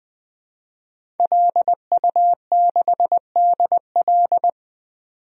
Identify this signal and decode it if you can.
Morse code, heard as a keyed tone, about 20 wpm, 700 Hz. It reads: LU6DL
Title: DXCC-100 Call Signs - 5 Characters - 20wpm